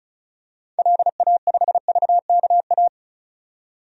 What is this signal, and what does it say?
Morse code, heard as a keyed tone, about 36 wpm, 700 Hz. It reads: LA5VKA
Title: DXCC-100 Call Signs - 6 Characters - 35wpm